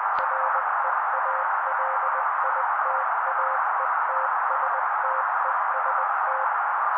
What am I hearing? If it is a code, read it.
Morse code, heard as a keyed tone, about 20 wpm, 530 Hz. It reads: REALITAETSTEST
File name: daily / 18wpm noise